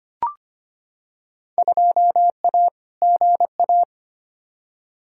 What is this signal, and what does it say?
Morse code, heard as a keyed tone, about 25 wpm, 700 Hz. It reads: 2A GA
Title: Field Day Exchanges Encore - 25wpm